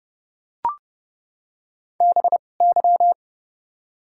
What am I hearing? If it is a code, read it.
Morse code, heard as a keyed tone, about 30 wpm, spaced as for 22 wpm, 700 Hz. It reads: BY
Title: 2x Character Spacing - Top 100 Words - 30wpm